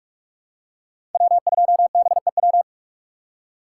Morse code, heard as a keyed tone, about 46 wpm, 700 Hz. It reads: W2BEW
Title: USA Call Signs - 5 Characters - 3x Character Spacing - 45wpm